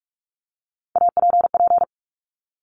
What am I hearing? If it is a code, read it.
Morse code, heard as a keyed tone, about 46 wpm, 690 Hz. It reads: APP
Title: Top Three Letter Combinations in Words 45wpm